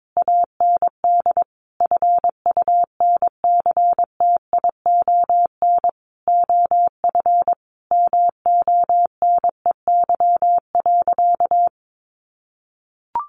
Morse code, heard as a keyed tone, about 22 wpm, 700 Hz. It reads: AND FUNCTION OF MONEY.
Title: Sentences from Top 1100 Words 22wpm